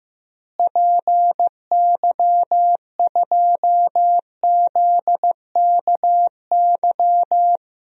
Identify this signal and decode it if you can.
Morse code, heard as a keyed tone, about 16 wpm, 700 Hz. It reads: PY2ZKY
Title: DXCC-100 Call Signs - 6 Characters - Repeated 3x - 15wpm